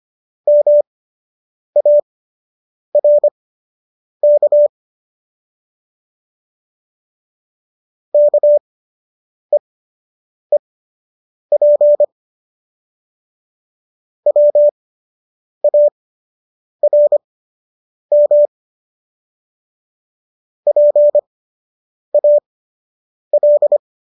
Morse code, heard as a keyed tone, about 25 wpm, spaced as for 8 wpm, 590 Hz. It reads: MARK KEEP WARM PAL